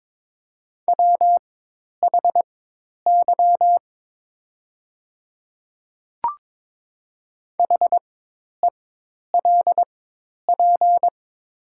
Morse code, heard as a keyed tone, about 22 wpm, spaced as for 10 wpm, 700 Hz. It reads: WHY HELP
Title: 4x Character Spacing - Top 200 Words - 22wpm